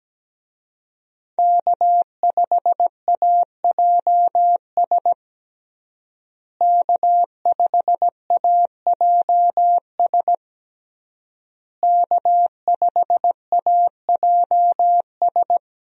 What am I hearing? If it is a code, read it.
Morse code, heard as a keyed tone, about 17 wpm, 700 Hz. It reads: K5AJS K5AJS K5AJS
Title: USA Call Signs - 5 Characters - Repeated 3x - 17wpm